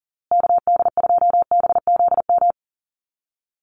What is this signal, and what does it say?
Morse code, heard as a keyed tone, about 40 wpm, 700 Hz. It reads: KD2BZM
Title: USA Call Signs Repeated 2x 40wpm